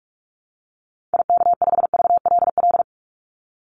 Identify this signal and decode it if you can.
Morse code, heard as a keyed tone, about 44 wpm, 700 Hz. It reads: IK5VLL
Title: Call Signs - 45wpm